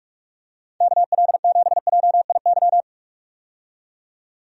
Morse code, heard as a keyed tone, about 43 wpm, 700 Hz. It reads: KL7JIY
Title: USA Call Signs - 6 Characters - 45wpm